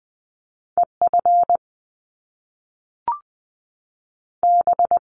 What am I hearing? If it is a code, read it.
Morse code, heard as a keyed tone, about 20 wpm, 700 Hz. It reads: EF B
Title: Top Two Letter Combinations in Words 20wpm